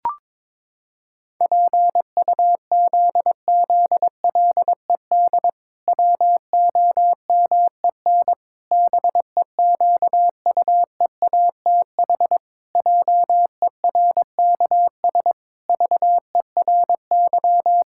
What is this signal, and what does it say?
Morse code, heard as a keyed tone, about 22 wpm, 700 Hz. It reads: PUZZLED WOMEN BEQUEATH JERKS VERY
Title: Pangram Challenge v2 - 22wpm